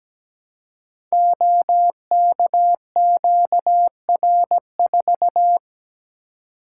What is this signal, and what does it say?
Morse code, heard as a keyed tone, about 17 wpm, 700 Hz. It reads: OKQR4
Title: Code Groups - 5 Letter:Numbers - 17wpm